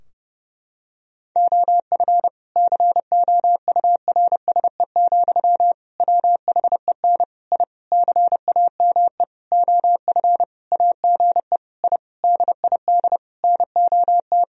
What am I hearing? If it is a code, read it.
Morse code, heard as a keyed tone, about 31 wpm, 700 Hz. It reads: OF COURSE, WHEN I CAME OF AGE I DID NOT
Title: Sentences from Top 1000 Words - Review Part 3 - 30wpm